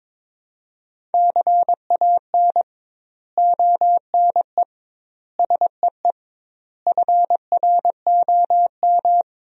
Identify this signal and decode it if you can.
Morse code, heard as a keyed tone, about 22 wpm, 700 Hz. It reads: CAN ONE SEE FROM
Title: 2x Word-Spacing - Sentences from Top 100 Words - 22wpm